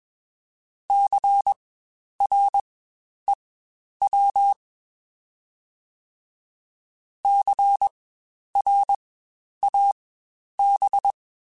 Morse code, heard as a keyed tone, about 21 wpm, spaced as for 10 wpm, 800 Hz. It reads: CREW CRAB
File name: Spaced Alphas / Spaced Alphas mp3s / Cs sp @20